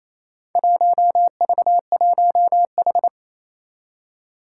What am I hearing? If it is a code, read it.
Morse code, heard as a keyed tone, about 28 wpm, 700 Hz. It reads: 1V1H
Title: Four Challenging Letter-Number Combinations 28wpm